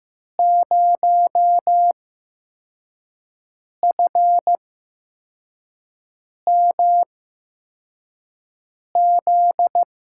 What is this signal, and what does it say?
Morse code, heard as a keyed tone, about 15 wpm, spaced as for 4 wpm, 700 Hz. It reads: 0FMZ
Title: USA Call Signs - 6 Characters - 8x Character Spacing - 15wpm